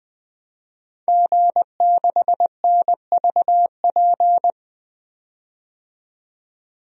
Morse code, heard as a keyed tone, about 20 wpm, 700 Hz. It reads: G6NVP
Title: DXCC-100 Call Signs - 5 Characters - 20wpm